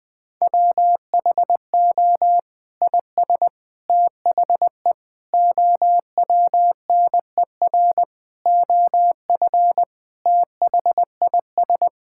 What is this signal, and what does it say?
Morse code, heard as a keyed tone, about 20 wpm, 700 Hz. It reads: WHO IS THE OWNER OF THIS
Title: Sentences from Top 1600 Words 20wpm